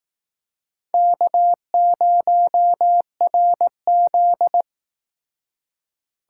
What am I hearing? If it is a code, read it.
Morse code, heard as a keyed tone, about 19 wpm, 700 Hz. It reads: K0RZ